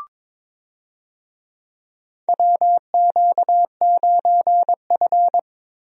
Morse code, heard as a keyed tone, about 22 wpm, 700 Hz. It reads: WQ9F
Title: USA Call Signs - 4 Characters - Repeated 2x - 22wpm